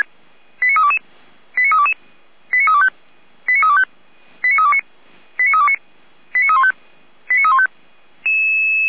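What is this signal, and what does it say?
Five-tone selective call (ZVEI), heard as quick runs of five tones, tones of about 70 ms. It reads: 89322 89322 89325 89325 89329 89329 89315 89315
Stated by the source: Received on an SDR as ZVEI five-tone call, address 89320.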